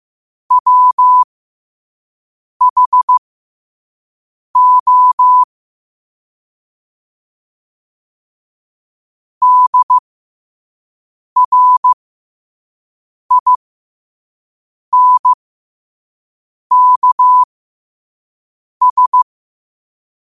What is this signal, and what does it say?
Morse code, heard as a keyed tone, about 15 wpm, spaced as for 5 wpm, 1000 Hz. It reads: WHO DRINKS
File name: qotdfiles / 5wpm